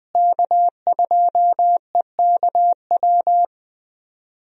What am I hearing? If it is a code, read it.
Morse code, heard as a keyed tone, about 20 wpm, 700 Hz. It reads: K2EKW
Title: USA Call Signs - 5 Characters - Repeated 3x - 20wpm